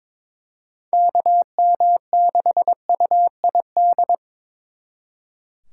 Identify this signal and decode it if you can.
Morse code, heard as a keyed tone, about 26 wpm, 700 Hz. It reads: KM6UID
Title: Speed Racing - USA 2x3 Call Signs 22wpm